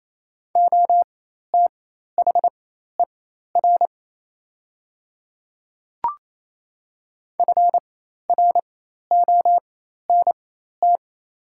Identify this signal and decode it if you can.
Morse code, heard as a keyed tone, about 28 wpm, spaced as for 13 wpm, 690 Hz. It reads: OTHER FRONT
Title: 4x Character Spacing - Top 500 Words - 28wpm